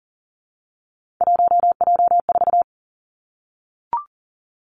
Morse code, heard as a keyed tone, about 40 wpm, 700 Hz. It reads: 1J4
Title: Three Challenging Letter-Number Combinations 40wpm